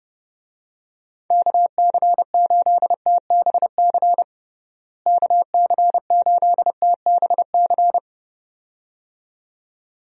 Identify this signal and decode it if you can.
Morse code, heard as a keyed tone, about 30 wpm, 700 Hz. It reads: KC8TBC KC8TBC
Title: USA Call Signs - 6 Characters - Repeated 2x - 30wpm